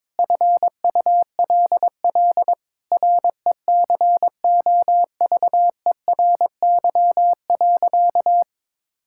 Morse code, heard as a keyed tone, about 22 wpm, 700 Hz. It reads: FULL RECOVERY.
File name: Sentences with 7-Word Phrases - 22wpm